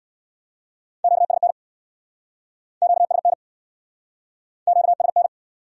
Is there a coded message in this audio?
Yes: Morse code from 1.0 to 5.3 s, heard as a keyed tone, about 66 wpm, 700 Hz. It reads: QSN QSN QSN